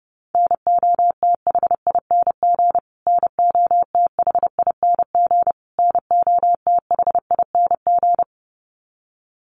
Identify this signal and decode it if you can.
Morse code, heard as a keyed tone, about 30 wpm, 700 Hz. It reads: NOTHING NOTHING NOTHING
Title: Top 500 Words Repeated 30wpm